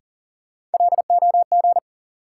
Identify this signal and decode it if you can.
Morse code, heard as a keyed tone, about 40 wpm, 700 Hz. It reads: LOG
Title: Sentences from Top 1400 Words - New Word Drill - 40wpm